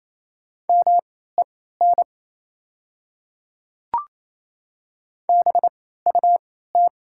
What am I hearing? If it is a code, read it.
Morse code, heard as a keyed tone, about 28 wpm, spaced as for 16 wpm, 700 Hz. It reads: MEN BUT
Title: 3x Character Spacing - Top 200 Words - 28wpm